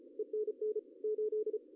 Morse code, heard as a keyed tone, about 32 wpm, 400 Hz, with radio noise. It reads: ENN 8
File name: CW / cqww-cw-k5tr